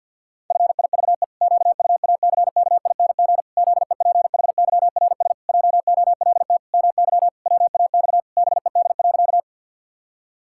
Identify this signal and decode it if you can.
Morse code, heard as a keyed tone, about 50 wpm, 700 Hz. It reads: FIVE QUACKING ZEPHYRS JOLT MY WAX BED.
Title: Pangram Challenge v2 - 50wpm